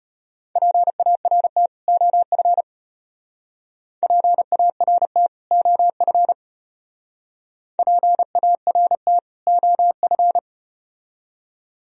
Morse code, heard as a keyed tone, about 34 wpm, 700 Hz. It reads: PART OF PART OF PART OF